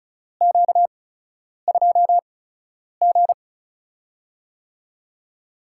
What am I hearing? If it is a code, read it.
Morse code, heard as a keyed tone, about 36 wpm, spaced as for 10 wpm, 700 Hz. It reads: Q2G
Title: USA Call Signs - 4 Characters - 8x Character Spacing - 35wpm